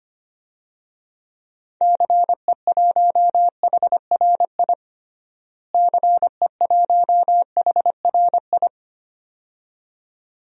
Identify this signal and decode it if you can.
Morse code, heard as a keyed tone, about 25 wpm, 700 Hz. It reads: CE1HRI CE1HRI
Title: DXCC-100 Call Signs - 6 Characters - Repeated 2x - 25wpm